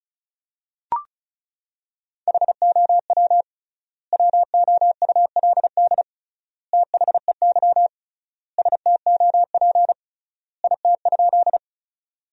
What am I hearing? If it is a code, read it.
Morse code, heard as a keyed tone, about 34 wpm, 700 Hz. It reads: HOW WOULD THEY STOP IT?
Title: 3x Word-Spacing - Sentences from Top 300 Words - 35wpm